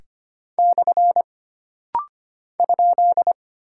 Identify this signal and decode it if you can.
Morse code, heard as a keyed tone, about 25 wpm, 700 Hz. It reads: / ?
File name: Single Punctuation-Prosign - Rapid-Fire - 25wpm